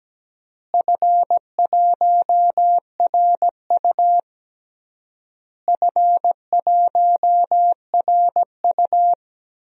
Morse code, heard as a keyed tone, about 17 wpm, 700 Hz. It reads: F1RU F1RU
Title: DXCC-100 Call Signs - 4 Characters - Repeated 3x - 17wpm